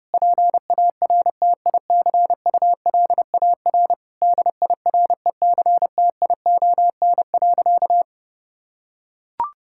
Morse code, heard as a keyed tone, about 30 wpm, 700 Hz. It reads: PARTICULAR DIRECTION.
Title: Sentences from Top 1000 Words - Review Part 5 - 30wpm